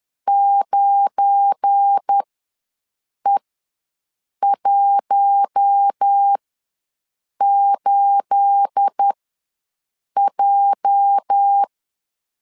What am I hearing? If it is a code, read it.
Morse code, heard as a keyed tone, about 10 wpm, spaced as for 6 wpm, 800 Hz. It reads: 9E18J
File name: LEZIONI / 9 286 ws 10wpm 4032